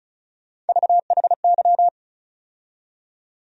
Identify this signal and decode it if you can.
Morse code, heard as a keyed tone, about 34 wpm, 700 Hz. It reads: VHY
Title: USA Call Sign Postfixes - 3 Letters 35wpm